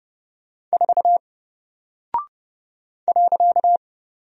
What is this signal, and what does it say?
Morse code, heard as a keyed tone, about 30 wpm, 700 Hz. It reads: 4 .
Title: Single Character (Letter:Number:Punctuation:Prosign) 30wpm